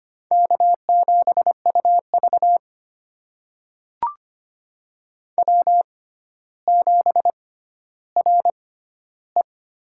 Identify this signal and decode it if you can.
Morse code, heard as a keyed tone, about 25 wpm, spaced as for 13 wpm, 700 Hz. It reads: K7UV W7RE
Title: USA Call Signs - 4 Characters - 6x Character Spacing - 25wpm